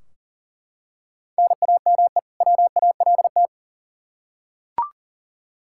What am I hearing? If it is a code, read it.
Morse code, heard as a keyed tone, about 40 wpm, 700 Hz. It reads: NAME WALT
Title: Practice QSO - Exchange 1 - Beginning - 40wpm